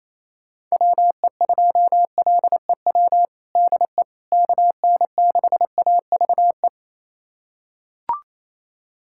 Morse code, heard as a keyed tone, about 28 wpm, 700 Hz. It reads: WE2LEW DE KN6AVE
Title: USA Callsign DE Callsign - 6 Characters - 28wpm